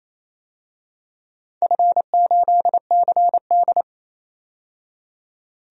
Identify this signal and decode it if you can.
Morse code, heard as a keyed tone, about 28 wpm, 700 Hz. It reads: F8CD
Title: DXCC-100 Call Signs - 4 Characters - 28wpm